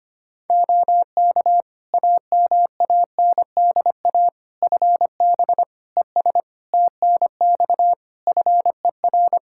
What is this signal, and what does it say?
Morse code, heard as a keyed tone, about 25 wpm, 700 Hz. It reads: OK AMANDA FB ES TNX FER
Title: K1USN SST QSO Practice 25wpm